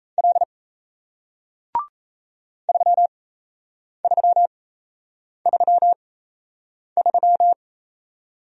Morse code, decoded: L 3333

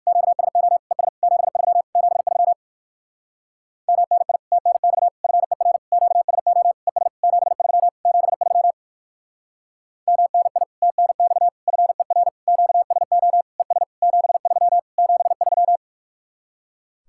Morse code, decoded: QSO ES 73 73 MNI TNX FER QSO ES 73 73 MNI TNX FER QSO ES 73 73

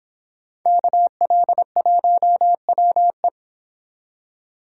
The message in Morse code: KL1WE

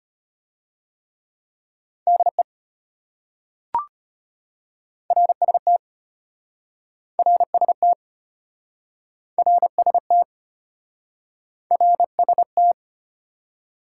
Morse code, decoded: DE RST RST RST RST

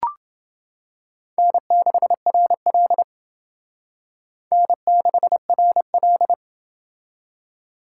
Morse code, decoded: N6RL N6RL

